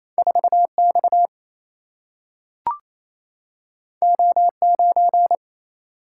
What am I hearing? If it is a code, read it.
Morse code, heard as a keyed tone, about 28 wpm, 700 Hz. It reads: 4X O9
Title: Single-Letter Single-Number 28wpm